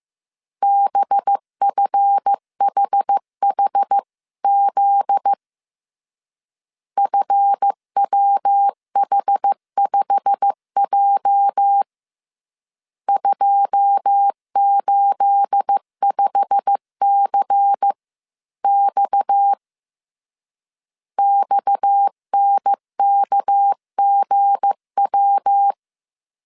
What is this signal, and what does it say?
Morse code, heard as a keyed tone, about 15 wpm, 800 Hz. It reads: BFHHZ FWH5J 285CX XNKGW
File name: CW / DET / 1 9 1500C S C 14wpm 3245